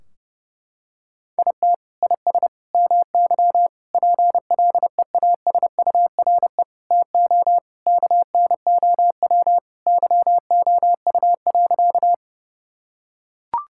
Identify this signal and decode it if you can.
Morse code, heard as a keyed tone, about 30 wpm, 700 Hz. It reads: IT IS MY PLEASURE TO KNOW YOU.
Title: Sentences from Top 1600 Words 30wpm